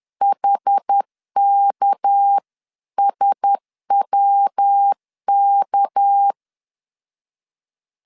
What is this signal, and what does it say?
Morse code, heard as a keyed tone, about 10 wpm, 800 Hz. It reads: HKSWK
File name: LEZIONI / 6 PKX0 / 6 PK 10wpm 2436